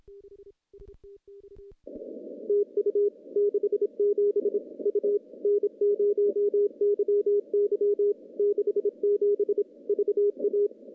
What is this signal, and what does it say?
Morse code, heard as a keyed tone, about 27 wpm, 410 Hz. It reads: TU 67 U N0YY 67 VA
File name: contests / CW-SS-k5tr- -0200